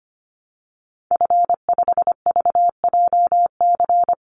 Morse code, heard as a keyed tone, about 25 wpm, 700 Hz. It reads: F5VJC
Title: Call Signs Repeated 3x - 25wpm